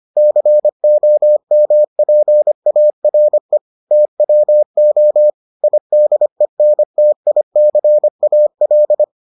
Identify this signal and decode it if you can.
Morse code, heard as a keyed tone, about 25 wpm, 600 Hz. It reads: COMPARE TWO IDENTICAL